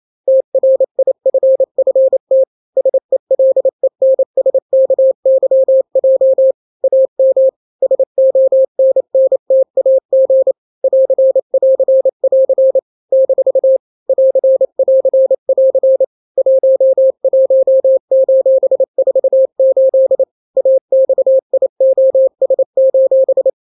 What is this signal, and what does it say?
Morse code, heard as a keyed tone, about 28 wpm, 530 Hz. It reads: TRIFFT SELENSKYJ AM SONNTAG <AR><AR><AR> - <AR><AR><AR> 11:48 AXIOS:
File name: daily / 27wpm